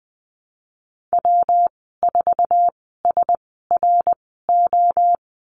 Morse code, heard as a keyed tone, about 20 wpm, spaced as for 15 wpm, 700 Hz. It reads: W4SRO
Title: 173 Lesson 28g - Number 9 - Call Signs 20wpm